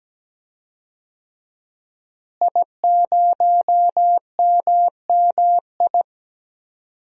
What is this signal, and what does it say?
Morse code, heard as a keyed tone, about 17 wpm, 700 Hz. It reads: I0MMI
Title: DXCC-100 Call Signs - 5 Characters - Repeated 3x - 17wpm